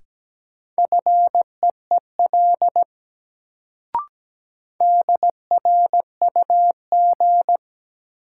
Morse code, heard as a keyed tone, about 17 wpm, 700 Hz. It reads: FEEL DRUG
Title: Sets of 5 Words - New Word Drill - 17wpm